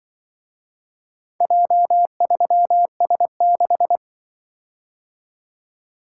Morse code, heard as a keyed tone, about 25 wpm, 700 Hz. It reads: J3S6
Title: Speed-Racing - Four Letter-Number Combinations - 20wpm